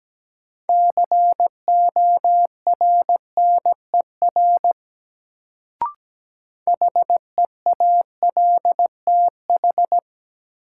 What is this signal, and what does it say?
Morse code, heard as a keyed tone, about 17 wpm, 700 Hz. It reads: CORNER HEALTH
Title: Sets of 5 Words - New Word Drill - 17wpm